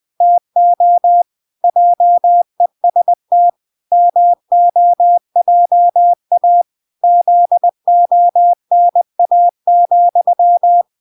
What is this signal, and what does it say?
Morse code, heard as a keyed tone, about 20 wpm, 700 Hz. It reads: TO JEST MOJA ZONA,